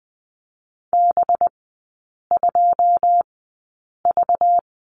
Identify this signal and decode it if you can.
Morse code, heard as a keyed tone, about 20 wpm, spaced as for 8 wpm, 690 Hz. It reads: B2V
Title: 191 Lesson 31e - Letter v - Three Characters 20wpm